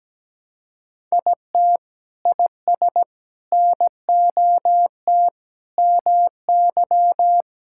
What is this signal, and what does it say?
Morse code, heard as a keyed tone, about 17 wpm, 700 Hz. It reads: IT IS NOT MY
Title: Sentences from Top 1000 Words - Review Part 5 - 17wpm